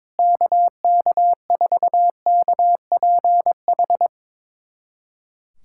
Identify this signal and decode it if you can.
Morse code, heard as a keyed tone, about 22 wpm, 700 Hz. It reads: KK4KPH
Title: USA Callsign DE Callsign 22wpm